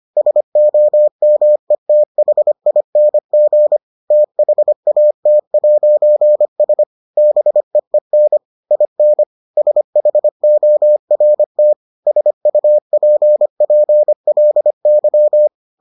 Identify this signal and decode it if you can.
Morse code, heard as a keyed tone, about 25 wpm, 600 Hz. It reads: SOMETHING THAT'S BEEN IN SHORT SUPPLY